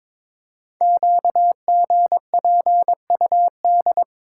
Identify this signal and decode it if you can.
Morse code, heard as a keyed tone, about 22 wpm, 700 Hz. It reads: QGPUD
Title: Code Groups - 5 Letters - 22wpm